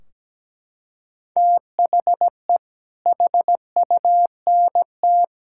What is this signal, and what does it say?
Morse code, heard as a keyed tone, about 17 wpm, 700 Hz. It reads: THE HUNT
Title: Sentences from Top 1600 Words 17wpm